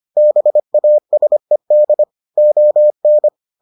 Morse code, heard as a keyed tone, about 25 wpm, 600 Hz. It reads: BASED ON